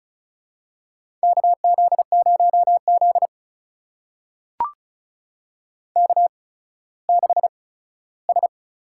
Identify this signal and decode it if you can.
Morse code, heard as a keyed tone, about 36 wpm, spaced as for 17 wpm, 700 Hz. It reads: KZ0Z K6S
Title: USA Call Signs - 4 Characters - 8x Character Spacing - 35wpm